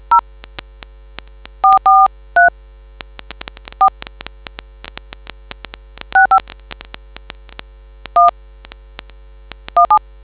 DTMF, heard as telephone touch tones, tones of about 120 ms. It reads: 0 443 4 65 1 17